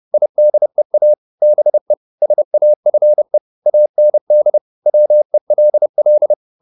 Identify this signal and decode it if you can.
Morse code, heard as a keyed tone, about 30 wpm, 600 Hz. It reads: IDEA BE SAFE AND WELL